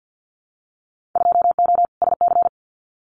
Morse code, heard as a keyed tone, about 50 wpm, 700 Hz. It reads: 2O SC
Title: Winter Field Day Exchanges 50wpm